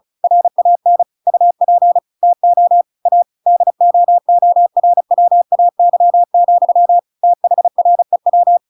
Morse code, heard as a keyed tone, about 34 wpm, 700 Hz. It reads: RAN UP TO A DOORWAY, THREW